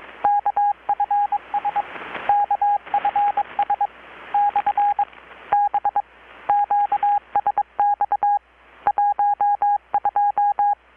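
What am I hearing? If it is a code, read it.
Morse code, heard as a keyed tone, about 22 wpm, spaced as for 19 wpm, 820 Hz, with radio noise. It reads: KFS KFS/B QSX 12
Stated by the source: Morse from KPH's Night of Nights XX maritime radiotelegraphy event, received in Chiapas, Mexico about 3,800 km away on a Kenwood R-600 with a grounded 10 m long-wire antenna.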